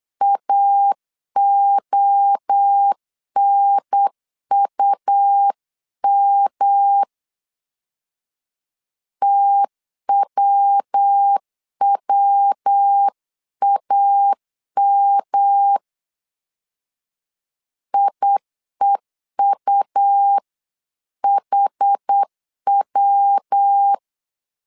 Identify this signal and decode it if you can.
Morse code, heard as a keyed tone, about 8 wpm, 800 Hz. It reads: AONUM TWWAM IEUHW